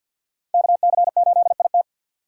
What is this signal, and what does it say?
Morse code, heard as a keyed tone, about 53 wpm, 700 Hz. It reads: KC8IT
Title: USA Call Signs - 4x Character Spacing - 50wpm